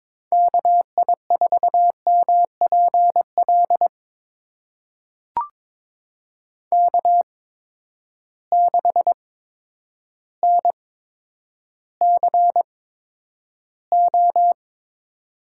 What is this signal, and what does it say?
Morse code, heard as a keyed tone, about 21 wpm, spaced as for 10 wpm, 700 Hz. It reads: KI4MPL K6NCO
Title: USA Call Signs - 8x Character Spacing - 22wpm